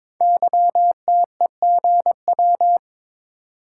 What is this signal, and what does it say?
Morse code, heard as a keyed tone, about 23 wpm, 700 Hz. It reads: YTEGW
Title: Code Groups - 5 Letter:Numbers - 22wpm